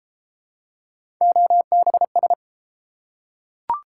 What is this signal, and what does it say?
Morse code, heard as a keyed tone, about 32 wpm, 700 Hz. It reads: OBS